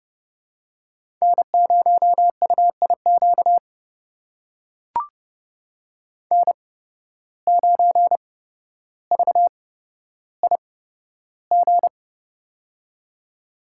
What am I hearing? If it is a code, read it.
Morse code, heard as a keyed tone, about 30 wpm, spaced as for 13 wpm, 700 Hz. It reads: N0UIQ N9VIG
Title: USA Call Signs - 5 Characters - 8x Character Spacing - 30wpm